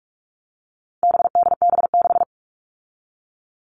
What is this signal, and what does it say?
Morse code, heard as a keyed tone, about 44 wpm, 700 Hz. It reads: BDB6
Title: All Cut Numbers - 4-Digit 45wpm